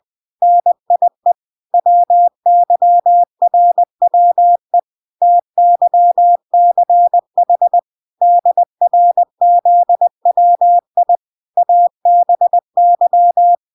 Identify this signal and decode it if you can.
Morse code, heard as a keyed tone, about 20 wpm, 700 Hz. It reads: NIE WYRWE TYCH DRZWI ABY